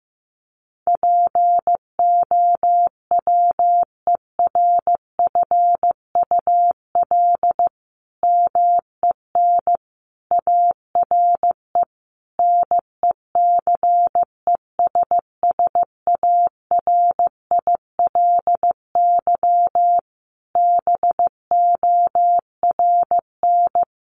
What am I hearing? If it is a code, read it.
Morse code, heard as a keyed tone, about 15 wpm, 700 Hz. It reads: POWERFUL MEN ARE NECESSARILY BORN
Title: Sentences from Top 1400 Words 15wpm